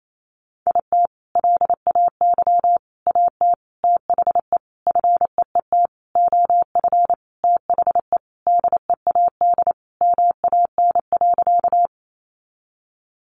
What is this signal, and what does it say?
Morse code, heard as a keyed tone, about 28 wpm, 700 Hz. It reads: IT LAY AT THE FEET OF THE DEAD MAN.
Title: Sentences from Top 1500 Words 28wpm